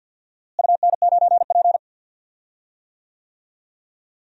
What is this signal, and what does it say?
Morse code, heard as a keyed tone, about 48 wpm, 700 Hz. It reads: UN9P
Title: DXCC-100 Call Signs - 4 Characters - Repeated 2x - 50wpm